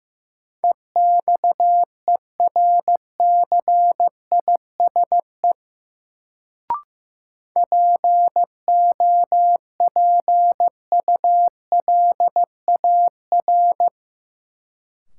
Morse code, heard as a keyed tone, about 15 wpm, 700 Hz. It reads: EXERCISE POPULAR
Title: Sentences from Top 1200 Words - New Word Drill Alt - 15wpm